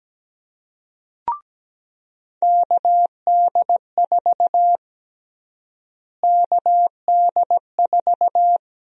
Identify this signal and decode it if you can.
Morse code, heard as a keyed tone, about 17 wpm, 700 Hz. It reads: KD4 KD4
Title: USA Call Sign Prefixes - Repeated 2x - 17wpm